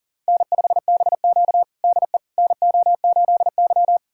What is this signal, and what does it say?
Morse code, heard as a keyed tone, about 37 wpm, 700 Hz. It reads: N5BQ DE NO8Y
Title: USA Callsign DE Callsign - 4 Characters - 40wpm